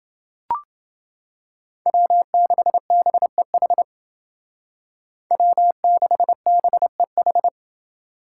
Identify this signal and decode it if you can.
Morse code, heard as a keyed tone, about 29 wpm, 700 Hz. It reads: W6BEH W6BEH